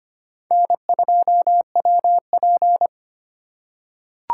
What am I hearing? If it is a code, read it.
Morse code, heard as a keyed tone, about 25 wpm, 700 Hz. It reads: N2WP
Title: USA Call Signs - 4x Character Spacing - 25wpm